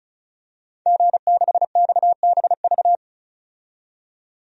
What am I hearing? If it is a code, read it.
Morse code, heard as a keyed tone, about 36 wpm, 700 Hz. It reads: G6XBV
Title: DXCC-100 Call Signs - 5 Characters - 35wpm